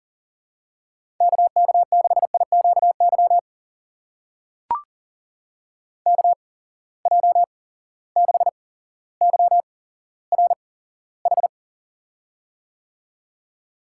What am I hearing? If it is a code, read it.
Morse code, heard as a keyed tone, about 40 wpm, spaced as for 17 wpm, 700 Hz. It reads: KK6IQY KJ6YRH
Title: USA Call Signs - 6 Characters - 8x Character Spacing - 40wpm